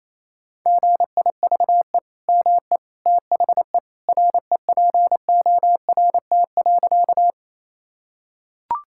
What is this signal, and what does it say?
Morse code, heard as a keyed tone, about 28 wpm, 700 Hz. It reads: GIVE ME THE REPORT.
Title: Sentences from Top 1000 Words - Review Part 6 - 28wpm